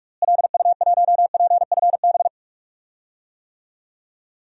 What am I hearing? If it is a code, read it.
Morse code, heard as a keyed tone, about 42 wpm, 700 Hz. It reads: LU1PFB
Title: DXCC-100 Call Signs - 6 Characters - Repeated 2x - 45wpm